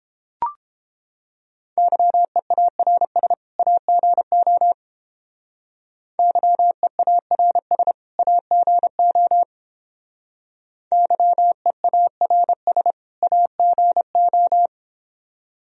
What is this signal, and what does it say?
Morse code, heard as a keyed tone, about 29 wpm, 700 Hz. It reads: YEARS AGO YEARS AGO YEARS AGO